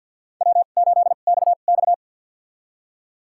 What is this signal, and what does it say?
Morse code, heard as a keyed tone, about 47 wpm, spaced as for 35 wpm, 690 Hz. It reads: W8XX
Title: USA Call Signs - 2x Character Spacing - 50wpm